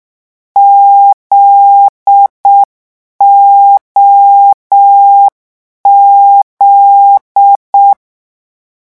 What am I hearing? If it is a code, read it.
Morse code, heard as a keyed tone, about 6 wpm, 800 Hz. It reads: ZOZ
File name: morse mp3 / lecon9-vitesse 6